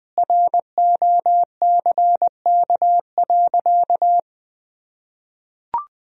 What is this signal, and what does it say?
Morse code, heard as a keyed tone, about 21 wpm, 700 Hz. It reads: ROCK.
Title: Sentences from Top 1000 Words - Review Part 2 - 20wpm